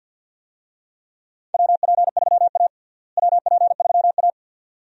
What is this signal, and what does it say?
Morse code, heard as a keyed tone, about 52 wpm, 700 Hz. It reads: WP3A WP3A
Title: USA Call Signs - 4 Characters - Repeated 2x - 50wpm